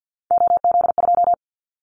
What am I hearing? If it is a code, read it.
Morse code, heard as a keyed tone, about 50 wpm, 700 Hz. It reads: OZ2
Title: Callsign Prefixes for Top 50 DXCC - 50wpm